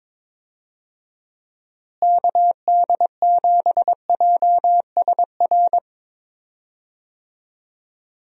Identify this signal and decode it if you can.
Morse code, heard as a keyed tone, about 22 wpm, 700 Hz. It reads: KD7JSR
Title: USA Call Signs - 6 Characters - Repeated 3x - 22wpm